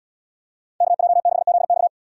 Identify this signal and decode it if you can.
Morse code, heard as a keyed tone, about 78 wpm, 690 Hz. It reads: KJ6ZX